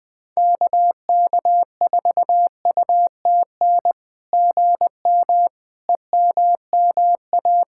Morse code, heard as a keyed tone, about 21 wpm, 700 Hz. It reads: KK4UTN GM EMMA
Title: SOTA QSO Practice 20wpm